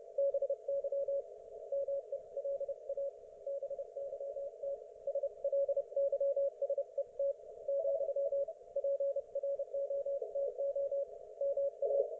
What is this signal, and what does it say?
Morse code, heard as a keyed tone, about 30 wpm, 560 Hz. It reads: BY MELA BYT SLYSET , PROTO MA
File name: Archiv / CUC208 / ok1fhi cul208